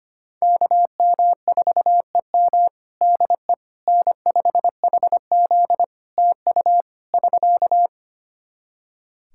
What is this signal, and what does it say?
Morse code, heard as a keyed tone, about 25 wpm, 700 Hz. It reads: KM4EM DE N5HZ TU <SK>
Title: Practice QSO - Exchange 4 - Ending - 25wpm